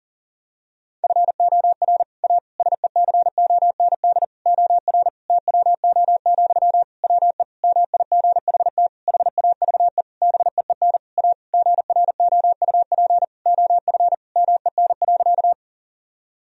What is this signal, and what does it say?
Morse code, heard as a keyed tone, about 40 wpm, 700 Hz. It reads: FOR A SECOND OR TWO, WE MIGHT HAVE BEEN A GROUP OF MEN.
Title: Sentences from Top 1000 Words - Review Part 1 - 40wpm